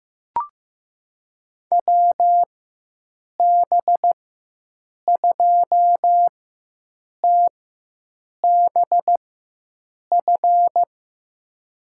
Morse code, heard as a keyed tone, about 15 wpm, spaced as for 7 wpm, 700 Hz. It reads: WB2TBF
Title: USA Call Signs - 6 Characters - 4x Character Spacing - 15wpm